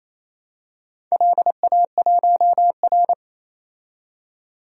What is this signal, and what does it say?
Morse code, heard as a keyed tone, about 28 wpm, 700 Hz. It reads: LA1R
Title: DXCC-100 Call Signs - 4 Characters - 28wpm